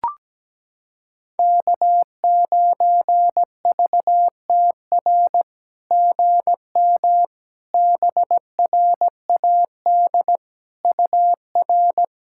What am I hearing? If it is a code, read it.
Morse code, heard as a keyed tone, about 17 wpm, 700 Hz. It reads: K9VTR GM BRAD UR
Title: SOTA QSO Practice 17wpm